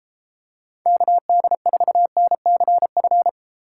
Morse code, heard as a keyed tone, about 33 wpm, 700 Hz. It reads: KD4NCF